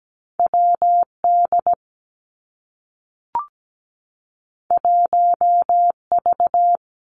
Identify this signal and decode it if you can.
Morse code, heard as a keyed tone, about 17 wpm, 700 Hz. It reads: WD 1V
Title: Two Challenging Letter-Number Combinations 17wpm